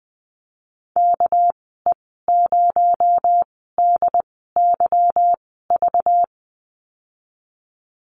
Morse code, heard as a keyed tone, about 20 wpm, spaced as for 15 wpm, 700 Hz. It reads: KE0DYV
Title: 219 Lesson 35g - Number 0 - Call Signs 20wpm